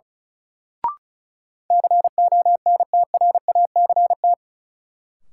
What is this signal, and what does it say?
Morse code, heard as a keyed tone, about 34 wpm, 700 Hz. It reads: CONTRACT
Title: Sentences from Top 1400 Words - New Word Drill Alt - 35wpm